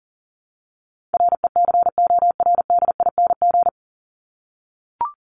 Morse code, heard as a keyed tone, about 40 wpm, 700 Hz. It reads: RECORDING
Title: Top 1501-1750 Words 40wpm